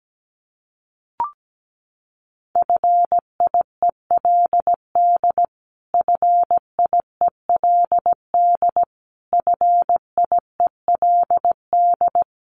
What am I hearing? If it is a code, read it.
Morse code, heard as a keyed tone, about 17 wpm, 700 Hz. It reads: FIELD FIELD FIELD
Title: Top 500 Words Repeated 17wpm